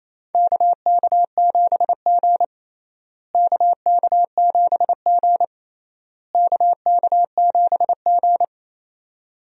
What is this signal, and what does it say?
Morse code, heard as a keyed tone, about 28 wpm, 700 Hz. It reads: KK7G KK7G KK7G
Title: USA Call Signs - 4 Characters - Repeated 3x - 28wpm